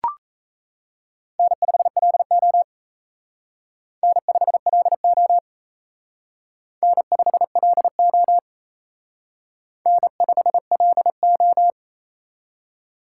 Morse code, heard as a keyed tone, about 34 wpm, 700 Hz. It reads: N5LO N5LO N5LO N5LO
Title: Speed Racing - USA 1x2 Call Signs 28wpm